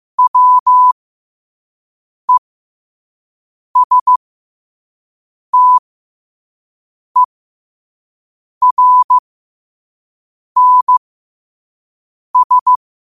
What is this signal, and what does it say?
Morse code, heard as a keyed tone, about 15 wpm, spaced as for 5 wpm, 1000 Hz. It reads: WESTERNS
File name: qotdfiles / 5wpm